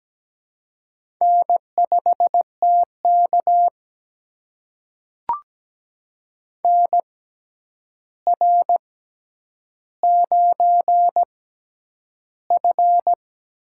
Morse code, heard as a keyed tone, about 17 wpm, spaced as for 9 wpm, 700 Hz. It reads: N5TK NR9F
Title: USA Call Signs - 4 Characters - 6x Character Spacing - 17wpm